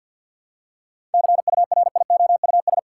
Morse code, heard as a keyed tone, about 52 wpm, 700 Hz. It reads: CURIOUS
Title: Seven Letter Words - 50wpm